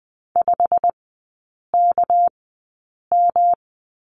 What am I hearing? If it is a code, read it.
Morse code, heard as a keyed tone, about 20 wpm, spaced as for 8 wpm, 700 Hz. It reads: 5KM
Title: 218 Lesson 35e - Number 0 - Three Characters 20wpm